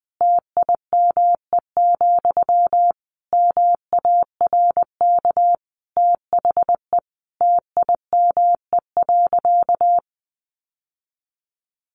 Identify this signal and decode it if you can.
Morse code, heard as a keyed tone, about 20 wpm, 700 Hz. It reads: TIME, MARK THE TIME.
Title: Sentences from Top 1400 Words 20wpm